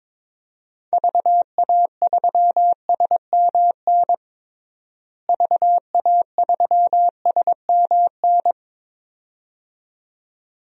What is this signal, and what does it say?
Morse code, heard as a keyed tone, about 22 wpm, 700 Hz. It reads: VA3SMN VA3SMN
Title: DXCC-100 Call Signs - 6 Characters - Repeated 2x - 22wpm